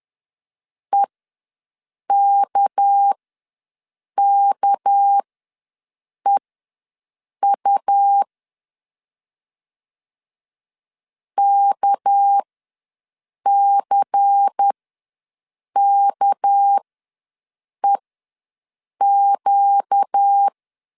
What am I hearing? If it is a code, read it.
Morse code, heard as a keyed tone, about 10 wpm, spaced as for 6 wpm, 800 Hz. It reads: EKKEU KCKEQ
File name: LEZIONI / 6 PKX0 / 6 PK ws 10wpm 3745